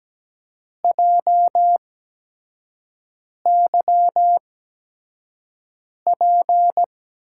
Morse code, heard as a keyed tone, about 17 wpm, spaced as for 5 wpm, 700 Hz. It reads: JYP
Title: USA Call Signs - 8x Character Spacing - 17wpm